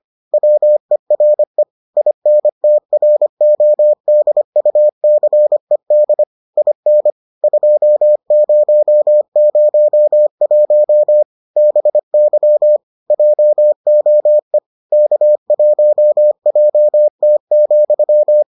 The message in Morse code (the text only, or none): WERE INTRODUCED IN 2001 BY JOE K1JT,